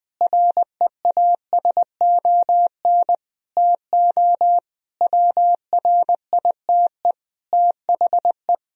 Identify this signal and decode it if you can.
Morse code, heard as a keyed tone, about 20 wpm, 700 Hz. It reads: REASON TO WRITE THE
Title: Sentences from Top 1000 Words - Review Part 1 - 20wpm